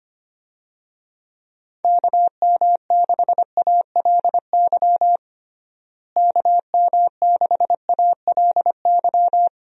Morse code, heard as a keyed tone, about 25 wpm, 700 Hz. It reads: KM6ALY KM6ALY
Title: USA Call Signs - 6 Characters - Repeated 2x - 25wpm